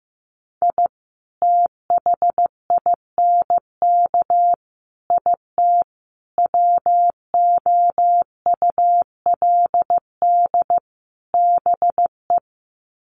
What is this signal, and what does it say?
Morse code, heard as a keyed tone, about 15 wpm, 700 Hz. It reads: I THINK IT WOULD BE
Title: Sets of 5 Words 15wpm